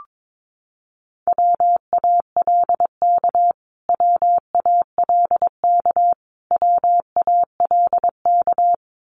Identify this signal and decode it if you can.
Morse code, heard as a keyed tone, about 22 wpm, 700 Hz. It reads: WALK WALK WALK
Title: Top 500 Words Repeated 22wpm